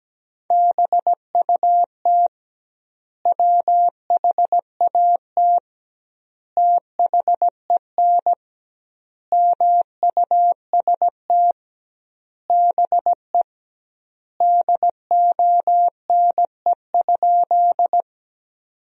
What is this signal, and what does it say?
Morse code, heard as a keyed tone, about 17 wpm, 700 Hz. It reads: BUT WHAT THEN MUST BE DONE?
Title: 2x Word-Spacing - Sentences from Top 500 Words - 17wpm